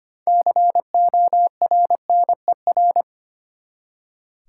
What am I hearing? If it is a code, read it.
Morse code, heard as a keyed tone, about 25 wpm, 700 Hz. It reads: CORNER
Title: Six Letter Words - 25wpm